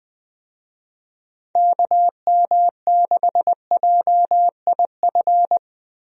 Morse code, heard as a keyed tone, about 20 wpm, 700 Hz. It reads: KM6JIF
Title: USA Call Signs - 6 Characters - Repeated 2x - 20wpm